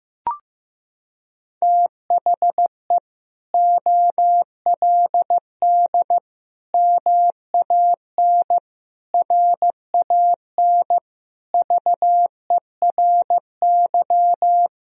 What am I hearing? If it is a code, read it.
Morse code, heard as a keyed tone, about 15 wpm, 700 Hz. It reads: THE OLD MAN RAN VERY
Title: Sentences from Top 1000 Words - Review Part 3 - 15wpm